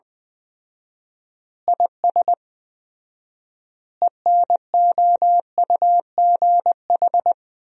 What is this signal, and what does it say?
Morse code, heard as a keyed tone, about 20 wpm, 700 Hz. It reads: IS ENOUGH
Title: 4x Word-Spacing - Sentences from Top 400 Words - 20wpm